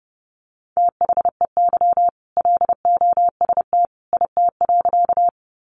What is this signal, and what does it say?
Morse code, heard as a keyed tone, about 30 wpm, 700 Hz. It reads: THEY LOST IT.
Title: Sentences from Top 1400 Words 30wpm